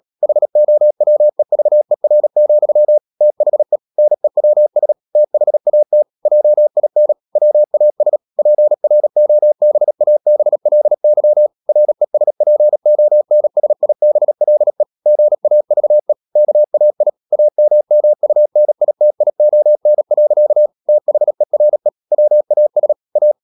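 Morse code, heard as a keyed tone, about 37 wpm, 600 Hz. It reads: HOWEVER, THE NEWS THAT JIN WAS PROBABLY RESPONSIBLE GAVE KAI AMMUNITION. THERE WAS A
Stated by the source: Morse code practice at 35–39 wpm sending English text of Ken Follett's novel Never, from CW med Gnister.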